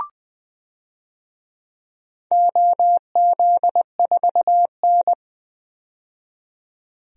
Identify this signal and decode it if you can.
Morse code, heard as a keyed tone, about 20 wpm, 700 Hz. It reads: OZ4N
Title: DXCC-100 Call Signs - 4 Characters - 20wpm